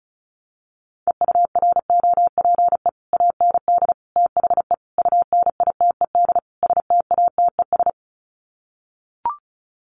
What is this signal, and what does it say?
Morse code, heard as a keyed tone, about 34 wpm, 700 Hz. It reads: EUROPE AND THE UNITED STATES
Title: Sets of 5 Words Encore2 35wpm